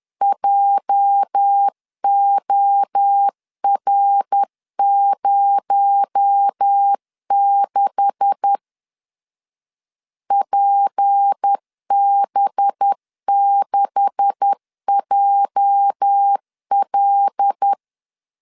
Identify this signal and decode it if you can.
Morse code, heard as a keyed tone, about 10 wpm, 800 Hz. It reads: JOR06 PB6JL
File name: LEZIONI / 9 286 10wpm 2735